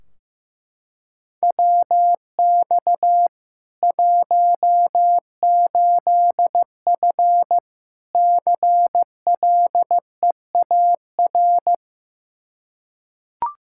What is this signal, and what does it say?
Morse code, heard as a keyed tone, about 15 wpm, 700 Hz. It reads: WX 18F CLEAR
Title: Practice QSO - Exchange 2 - Info - 15wpm